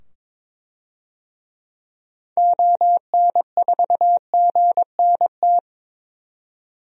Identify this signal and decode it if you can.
Morse code, heard as a keyed tone, about 22 wpm, 700 Hz. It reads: ON4GNT
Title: DXCC-100 Call Signs - 6 Characters - 22wpm